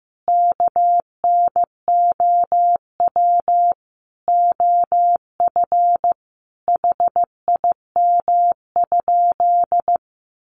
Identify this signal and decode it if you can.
Morse code, heard as a keyed tone, about 15 wpm, 700 Hz. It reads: KNOW OF HIM?
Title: Sentences from Top 100 Words 15wpm